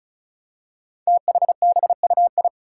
Morse code, heard as a keyed tone, about 36 wpm, 700 Hz. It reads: THBUI
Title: Five Challenging Letter-Number Combinations 35wpm